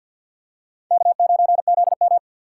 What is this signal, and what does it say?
Morse code, heard as a keyed tone, about 52 wpm, 700 Hz. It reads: K9ZM
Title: USA Call Signs - 4 Characters - 3x Character Spacing - 50wpm